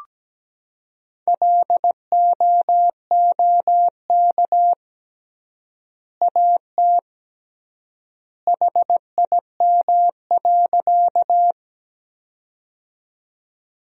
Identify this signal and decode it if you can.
Morse code, heard as a keyed tone, about 17 wpm, 700 Hz. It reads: LOOK AT HIM.
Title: 3x Word-Spacing - Sentences from Top 100 Words - 17wpm